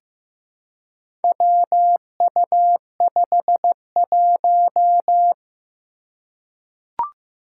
Morse code, heard as a keyed tone, about 15 wpm, 700 Hz. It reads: WU51
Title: Four Challenging Letter-Number Combinations 15wpm